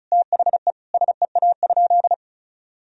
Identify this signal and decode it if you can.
Morse code, heard as a keyed tone, about 36 wpm, 700 Hz. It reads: THE SEA?
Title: Sentences from Top 1000 Words - Review Part 1 - 35wpm